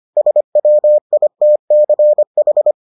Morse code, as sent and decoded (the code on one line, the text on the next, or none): ... .-- .. - -.-. ....
SWITCH